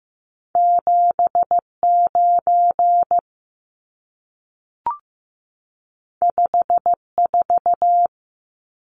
Morse code, heard as a keyed tone, about 15 wpm, 700 Hz. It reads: 79 54
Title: Numbers - 2-Digit 15wpm